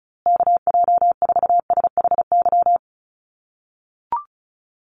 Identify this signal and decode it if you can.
Morse code, heard as a keyed tone, about 34 wpm, 700 Hz. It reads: KJ4SHY
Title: USA Call Signs Repeated 3x 35wpm